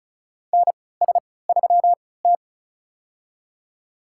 Morse code, heard as a keyed tone, about 34 wpm, spaced as for 20 wpm, 700 Hz. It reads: NS3T
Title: USA Call Signs - 3x Character Spacing - 35wpm